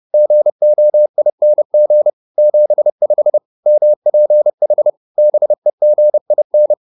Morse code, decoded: GOING 75 MPH BEGIN

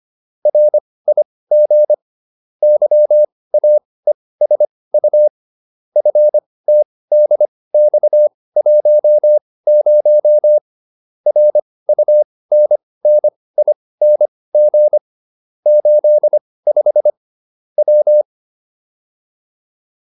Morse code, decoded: RIG YAESU FTDX10 RUNNING 85 W

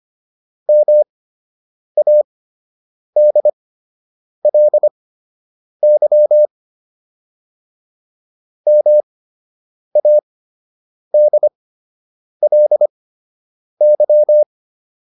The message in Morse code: MADLY MADLY

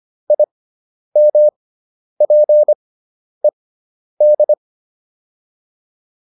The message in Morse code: IMPED

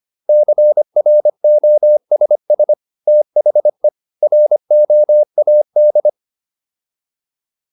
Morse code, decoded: CROSS THE ROAD

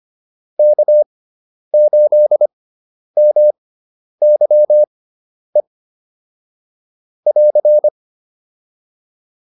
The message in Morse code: K8MYE <AR>

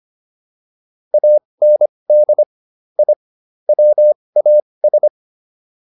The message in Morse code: AND I WAS

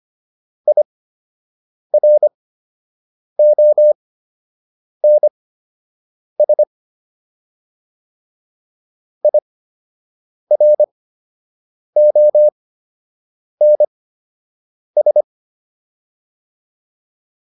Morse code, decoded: IRONS IRONS

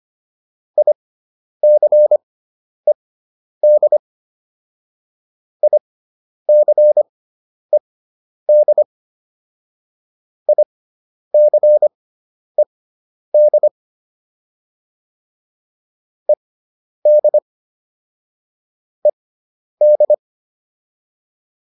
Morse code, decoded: ICED ICED ICED ED ED

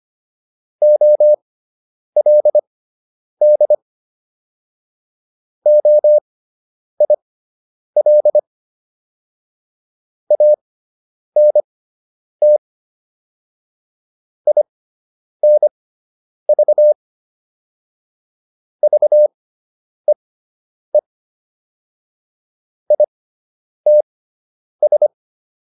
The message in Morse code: OLD OIL ANT INV VEE ITS

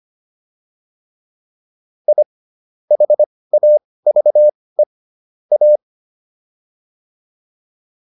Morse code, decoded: I HAVE A

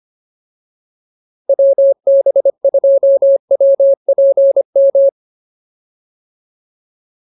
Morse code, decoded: WB2WPM

